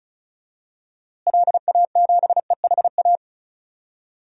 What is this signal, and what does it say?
Morse code, heard as a keyed tone, about 36 wpm, 700 Hz. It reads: LA7EHA
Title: DXCC-100 Call Signs - 6 Characters - Repeated 3x - 35wpm